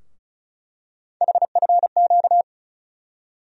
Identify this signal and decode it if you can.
Morse code, heard as a keyed tone, about 36 wpm, 700 Hz. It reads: HFQ
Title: USA Call Sign Postfixes - 3 Letters 35wpm